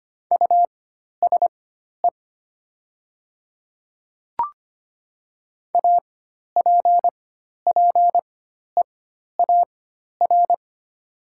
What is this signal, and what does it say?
Morse code, heard as a keyed tone, about 25 wpm, spaced as for 12 wpm, 700 Hz. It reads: USE APPEAR
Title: 4x Character Spacing - Top 500 Words - 25wpm